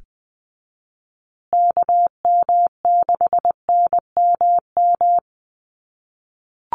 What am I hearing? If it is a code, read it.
Morse code, heard as a keyed tone, about 20 wpm, 700 Hz. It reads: KM6NMM
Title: USA Call Signs Repeated 2x 20wpm